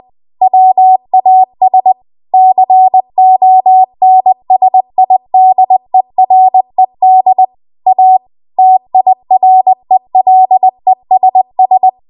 Morse code, heard as a keyed tone, about 20 wpm, 750 Hz. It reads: WAS CONSIDERED A TIRELESS
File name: Archive / 20 WPM / 20WPM